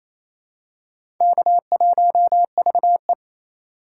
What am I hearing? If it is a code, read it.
Morse code, heard as a keyed tone, about 28 wpm, 700 Hz. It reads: K1VE
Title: USA Call Signs - 4x Character Spacing - 28wpm